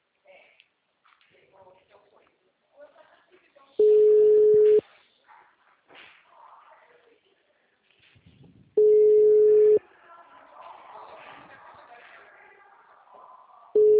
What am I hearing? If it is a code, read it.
Ringback tone (Europe, 425 Hz), with speech.
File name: out- -2012- - - .65865